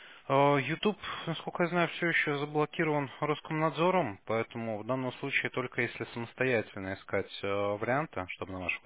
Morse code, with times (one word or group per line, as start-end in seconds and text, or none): none